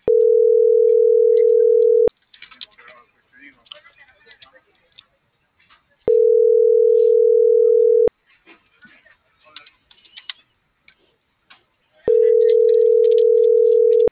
Ringback tone (North America), 440 + 480 Hz.